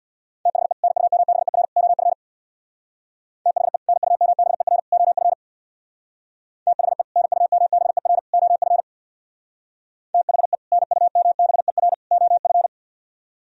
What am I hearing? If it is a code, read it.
Morse code, heard as a keyed tone, about 60 wpm, 700 Hz. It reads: THE NUMBER OF THE NUMBER OF THE NUMBER OF THE NUMBER OF
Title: Speed-Racing - Sets of 3 Words 50wpm